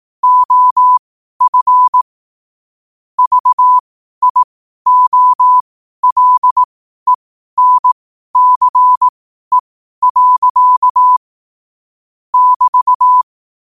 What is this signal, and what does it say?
Morse code, heard as a keyed tone, about 18 wpm, spaced as for 12 wpm, 1000 Hz. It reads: OF VIOLENCE. <BT>
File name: qotdfiles / 12wpm